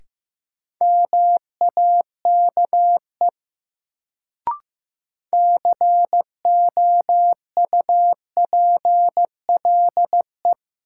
Morse code, heard as a keyed tone, about 15 wpm, 700 Hz. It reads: MAKE COUPLE
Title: Sets of 2 Words - New Word Drill - 15wpm